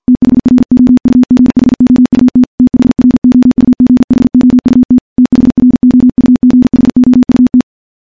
Morse code, heard as a keyed tone, about 48 wpm, 250 Hz. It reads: THROUGHOUT THROUGHOUT THROUGHOUT
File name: morse / morse-training-50WPM